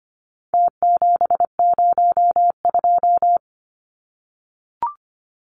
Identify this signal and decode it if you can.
Morse code, heard as a keyed tone, about 25 wpm, 700 Hz. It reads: T702
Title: Numbers - Contest Style - 4-Digit 25wpm